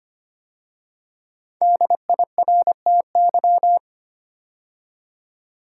Morse code, heard as a keyed tone, about 25 wpm, 700 Hz. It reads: DIRTY
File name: 5 Letter Words - First 3 Letters Given - 25wpm